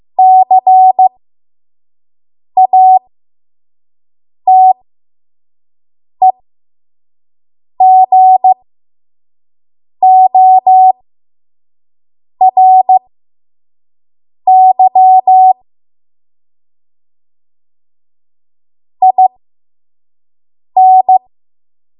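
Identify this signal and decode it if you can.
Morse code, heard as a keyed tone, about 15 wpm, spaced as for 5 wpm, 750 Hz. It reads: CATEGORY IN
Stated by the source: 5 WPM Morse code practice.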